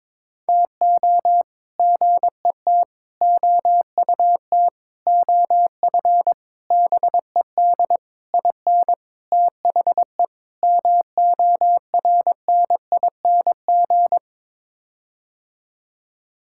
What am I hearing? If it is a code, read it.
Morse code, heard as a keyed tone, about 22 wpm, 700 Hz. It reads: TO GET OUT OF BED IN THE MORNING
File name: practice / 22wpm / Sets of 8 Words - 22wpm